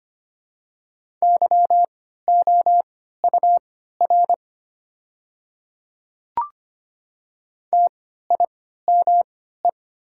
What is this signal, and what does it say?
Morse code, heard as a keyed tone, about 25 wpm, spaced as for 14 wpm, 700 Hz. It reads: YOUR TIME
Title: 3x Character Spacing - Top 100 Words - 25wpm